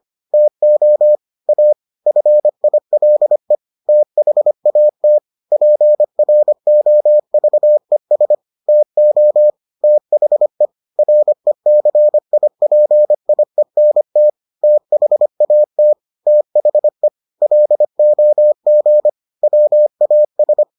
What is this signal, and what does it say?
Morse code, heard as a keyed tone, about 25 wpm, 600 Hz. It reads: TO A FILE THAT PROVES TO THE RECIPIENT THAT THE LOG WAS